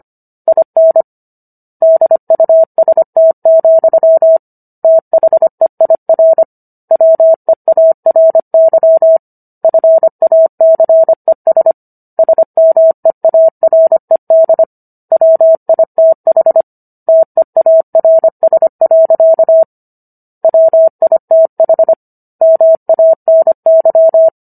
Morse code, heard as a keyed tone, about 25 wpm, 650 Hz. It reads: IN DUST, THEIR WEARY FACES SMEARED WITH TEARS. WITH MANY